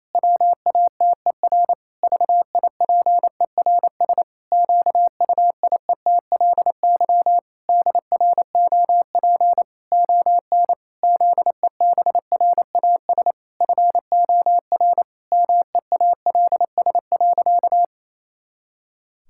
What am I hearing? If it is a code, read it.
Morse code, heard as a keyed tone, about 28 wpm, 700 Hz. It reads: WATER VIPERS QUIETLY DROP ON ZEBRAS FOR MEALS.
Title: Pangram Challenge v2 - 28wpm